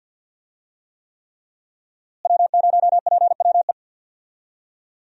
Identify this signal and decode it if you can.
Morse code, heard as a keyed tone, about 52 wpm, 700 Hz. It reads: W0PWE
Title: USA Call Signs - 5 Characters - Repeated 3x - 50wpm